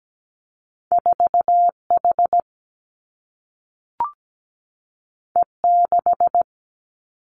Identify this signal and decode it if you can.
Morse code, heard as a keyed tone, about 17 wpm, 700 Hz. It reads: 4 H E 6
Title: Two Challenging Letter-Number Combinations 17wpm